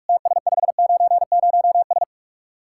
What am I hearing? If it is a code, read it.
Morse code, heard as a keyed tone, about 44 wpm, 700 Hz. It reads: TS590S